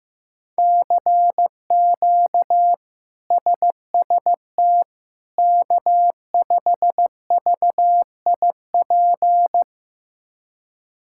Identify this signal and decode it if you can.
Morse code, heard as a keyed tone, about 15 wpm, 700 Hz. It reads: CQ SST K5VIP
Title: K1USN SST QSO Practice - 15wpm